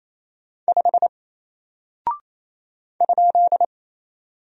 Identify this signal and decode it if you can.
Morse code, heard as a keyed tone, about 27 wpm, 700 Hz. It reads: 5 ?
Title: Single Character (Letter:Number:Punctuation:Prosign) Alt 28wpm